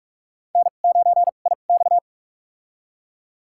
Morse code, heard as a keyed tone, about 42 wpm, spaced as for 31 wpm, 700 Hz. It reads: N9IX
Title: USA Call Signs - 4 Characters - 2x Character Spacing - 45wpm